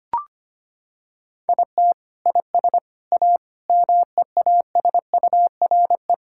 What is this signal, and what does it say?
Morse code, heard as a keyed tone, about 25 wpm, 700 Hz. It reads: IT IS A MEASURE
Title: Sentences from Top 1600 Words 25wpm